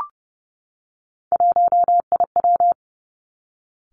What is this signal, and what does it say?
Morse code, heard as a keyed tone, about 30 wpm, 700 Hz. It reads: 1IW
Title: Three Challenging Letter-Number Combinations 30wpm